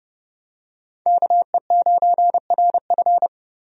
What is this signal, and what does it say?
Morse code, heard as a keyed tone, about 29 wpm, 700 Hz. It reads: KE9RF
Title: USA Call Signs - 5 Characters - 30wpm